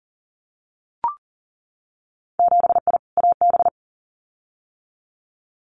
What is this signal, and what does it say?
Morse code, heard as a keyed tone, about 38 wpm, 700 Hz. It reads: 7I AB
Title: Winter Field Day Exchanges Encore 40wpm